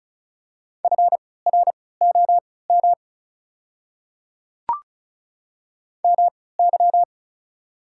Morse code, heard as a keyed tone, about 36 wpm, spaced as for 20 wpm, 690 Hz. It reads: FROM MY
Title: 3x Character Spacing - Top 100 Words - 35wpm